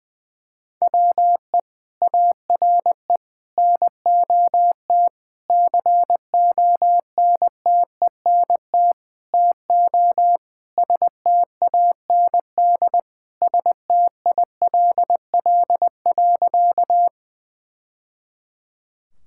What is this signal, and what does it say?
Morse code, heard as a keyed tone, about 20 wpm, 700 Hz. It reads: WE ARE NOT CONTENT TO STAND STILL.
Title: Sentences from Top 1000 Words - Review Part 6 - 20wpm